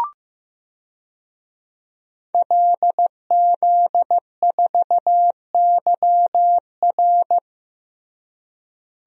Morse code, heard as a keyed tone, about 15 wpm, 700 Hz. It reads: LZ4YR
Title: DXCC-100 Call Signs - 5 Characters - Repeated 3x - 15wpm